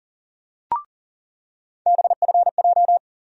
Morse code, heard as a keyed tone, about 40 wpm, 700 Hz. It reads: BFJ
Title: USA Call Sign Postfixes - 3 Letters 40wpm